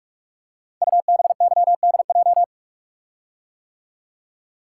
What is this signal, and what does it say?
Morse code, heard as a keyed tone, about 44 wpm, 700 Hz. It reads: UBYDJ
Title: Code Groups - 5 Letters - 45wpm